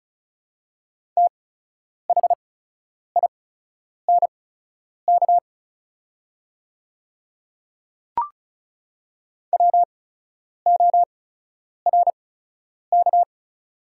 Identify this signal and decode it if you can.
Morse code, heard as a keyed tone, about 36 wpm, spaced as for 10 wpm, 700 Hz. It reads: THINK WORK
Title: 8x Character Spacing - Top 200 Words - 35wpm